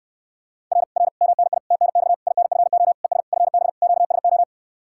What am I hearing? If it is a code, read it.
Morse code, heard as a keyed tone, about 69 wpm, 700 Hz. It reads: R R MNI TNX INFO ES FB QSO